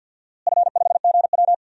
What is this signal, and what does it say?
Morse code, heard as a keyed tone, about 52 wpm, 700 Hz. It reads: F5GW
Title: DXCC-100 Call Signs - 4 Characters - Repeated 2x - 50wpm